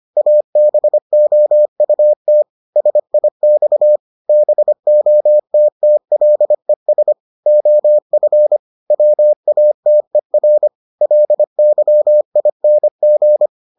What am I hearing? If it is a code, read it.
Morse code, heard as a keyed tone, about 25 wpm, 600 Hz. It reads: ABOUT SIX BOTTLES OF WATER LYING